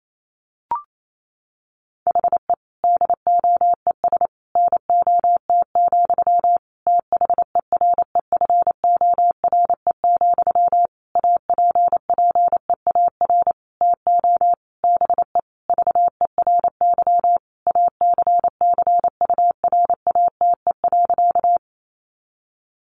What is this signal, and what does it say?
Morse code, heard as a keyed tone, about 28 wpm, 700 Hz. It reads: HE DOES NOT, THEREFORE, APPEAR TO BE VERY ACCURATE.
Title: Sentences from Top 1200 Words 28wpm